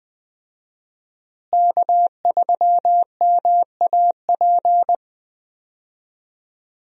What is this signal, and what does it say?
Morse code, heard as a keyed tone, about 20 wpm, 700 Hz. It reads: K3MAP
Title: USA Call Signs - 5 Characters - Repeated 2x - 20wpm